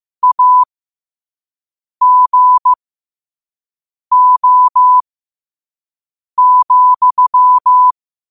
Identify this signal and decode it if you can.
Morse code, heard as a keyed tone, about 15 wpm, spaced as for 5 wpm, 1000 Hz. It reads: AGO,
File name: qotdfiles / 5wpm